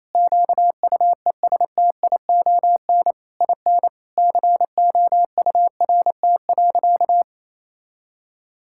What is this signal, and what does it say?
Morse code, heard as a keyed tone, about 28 wpm, 700 Hz. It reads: QUESTION IN COURT.
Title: Sentences from Top 1600 Words 28wpm